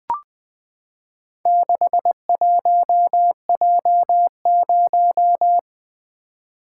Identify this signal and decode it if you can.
Morse code, heard as a keyed tone, about 20 wpm, 700 Hz. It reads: 61J0
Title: Four Letter-Number Combinations - 20wpm